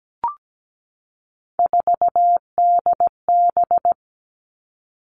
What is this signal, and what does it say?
Morse code, heard as a keyed tone, about 17 wpm, 700 Hz. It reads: 4DB
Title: All Cut Numbers - 3-Digit 17wpm